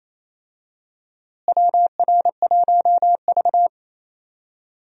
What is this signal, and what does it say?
Morse code, heard as a keyed tone, about 28 wpm, 700 Hz. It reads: WR1V
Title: USA Call Signs - 4 Characters - Repeated 2x - 28wpm